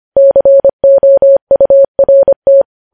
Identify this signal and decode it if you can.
Morse code, heard as a keyed tone, about 25 wpm, 550 Hz. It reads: COURT
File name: podcast / News headlines-1x-25x25